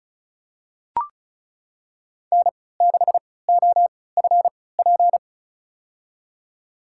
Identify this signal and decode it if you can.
Morse code, heard as a keyed tone, about 36 wpm, spaced as for 20 wpm, 700 Hz. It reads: N6OFP
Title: USA Call Signs - 5 Characters - 3x Character Spacing - 35wpm